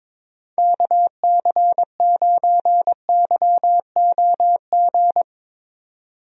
Morse code, heard as a keyed tone, about 22 wpm, 700 Hz. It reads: KC9YOG
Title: USA Call Signs - 6 Characters - 22wpm